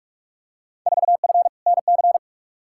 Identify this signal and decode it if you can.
Morse code, heard as a keyed tone, about 46 wpm, 700 Hz. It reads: 4F NC
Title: Field Day Exchanges - 45wpm